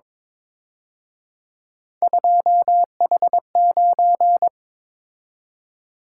Morse code, heard as a keyed tone, about 26 wpm, 700 Hz. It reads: 2H9
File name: Speed-Racing - Three Letter-Number Combinations - 22wpm